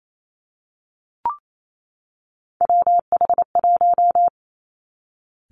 Morse code, heard as a keyed tone, about 28 wpm, 700 Hz. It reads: WH1
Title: Three Challenging Letter-Number Combinations 28wpm